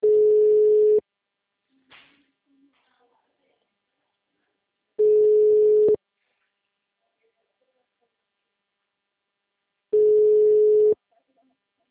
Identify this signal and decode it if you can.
Ringback tone (Europe, 425 Hz), with speech.
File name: out- -2006- - - .842